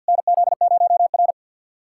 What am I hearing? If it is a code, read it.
Morse code, heard as a keyed tone, about 52 wpm, 690 Hz. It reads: NZ0R
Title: USA Callsign DE Callsign - 4 Characters - 50wpm